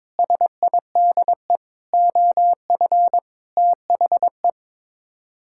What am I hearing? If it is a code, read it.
Morse code, heard as a keyed tone, about 22 wpm, 700 Hz. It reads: SIDE OF THE
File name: practice / 22wpm / Sets of 6 Words - 22wpm